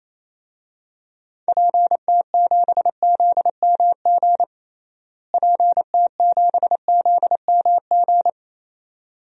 Morse code, heard as a keyed tone, about 28 wpm, 700 Hz. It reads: PT7ZMG PT7ZMG
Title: DXCC-100 Call Signs - 6 Characters - Repeated 2x - 28wpm